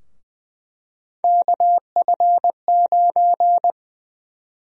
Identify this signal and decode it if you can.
Morse code, heard as a keyed tone, about 20 wpm, 700 Hz. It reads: KF9
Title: USA Call Sign Prefixes - 20wpm